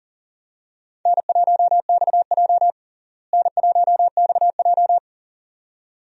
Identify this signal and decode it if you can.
Morse code, heard as a keyed tone, about 40 wpm, 700 Hz. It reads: N1XJ N1XJ
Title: USA Call Signs - 4 Characters - Repeated 2x - 40wpm